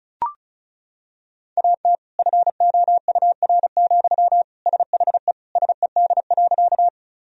Morse code, heard as a keyed tone, about 36 wpm, 700 Hz. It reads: AT FOUR, SHE SED.
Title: Sentences from Top 300 Words and Abbreviations 35wpm